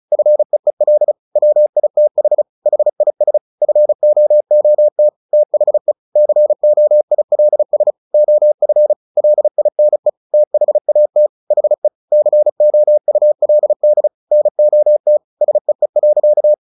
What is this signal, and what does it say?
Morse code, heard as a keyed tone, about 37 wpm, 600 Hz. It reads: FEEL WITH HIS FOOT THE COILS OF LINE THAT HE COULD NOT SEE.
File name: the old man and the sea 2 35wpm